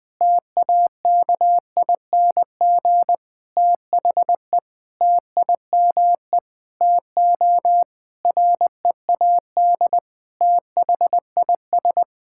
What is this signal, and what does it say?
Morse code, heard as a keyed tone, about 20 wpm, 700 Hz. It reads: TAKING THE TIME TO READ THIS
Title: Sentences with 9-Word Phrases - 20wpm